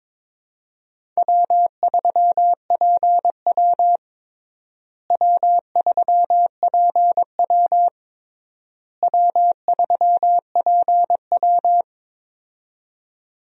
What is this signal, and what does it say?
Morse code, heard as a keyed tone, about 22 wpm, 700 Hz. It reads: W3PW W3PW W3PW
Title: USA Call Signs - 4 Characters - Repeated 3x - 22wpm